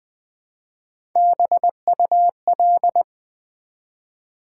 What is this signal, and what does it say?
Morse code, heard as a keyed tone, about 20 wpm, 700 Hz. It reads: BUL
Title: Build A Word - 1 Letter At A Time - 3 to 5-Letter Words - 20wpm